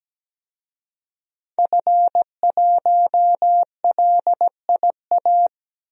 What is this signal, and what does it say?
Morse code, heard as a keyed tone, about 17 wpm, 700 Hz. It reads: F1LIA
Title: DXCC-100 Call Signs - 5 Characters - 17wpm